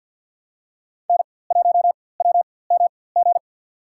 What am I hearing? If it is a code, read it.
Morse code, heard as a keyed tone, about 47 wpm, spaced as for 23 wpm, 700 Hz. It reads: N1WMG
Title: USA Call Signs - 5 Characters - 4x Character Spacing - 50wpm